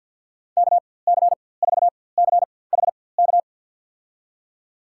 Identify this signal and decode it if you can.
Morse code, heard as a keyed tone, about 52 wpm, spaced as for 23 wpm, 700 Hz. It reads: KC4CHK
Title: USA Call Signs - 6 Characters - 4x Character Spacing - 50wpm